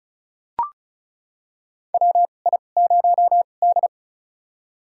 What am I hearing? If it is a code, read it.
Morse code, heard as a keyed tone, about 36 wpm, spaced as for 25 wpm, 700 Hz. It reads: WI0D
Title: USA Call Signs - 4 Characters - 2x Character Spacing - 35wpm